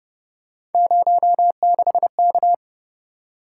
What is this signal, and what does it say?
Morse code, heard as a keyed tone, about 29 wpm, 700 Hz. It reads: 06K